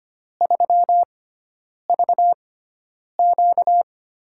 Morse code, heard as a keyed tone, about 25 wpm, spaced as for 9 wpm, 700 Hz. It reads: 3VQ
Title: USA Call Signs - 4 Characters - 6x Character Spacing - 25wpm